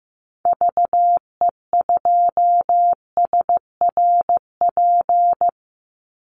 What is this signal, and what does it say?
Morse code, heard as a keyed tone, about 15 wpm, 700 Hz. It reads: VE2SRP
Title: Call Signs Repeated 3x - 15wpm